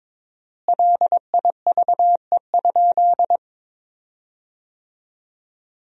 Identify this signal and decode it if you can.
Morse code, heard as a keyed tone, about 22 wpm, 700 Hz. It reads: LIVE?
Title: 4x Word-Spacing - Sentences from Top 200 Words - 22wpm